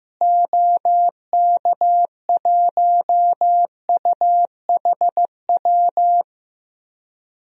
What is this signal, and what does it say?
Morse code, heard as a keyed tone, about 16 wpm, 700 Hz. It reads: OK1UHW
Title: DXCC-100 Call Signs - 6 Characters - Repeated 3x - 15wpm